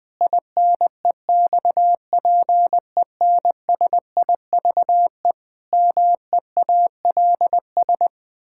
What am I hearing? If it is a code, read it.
Morse code, heard as a keyed tone, about 20 wpm, 700 Hz. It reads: INEXPENSIVE MEALS
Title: Pangram Challenge v2 - 20wpm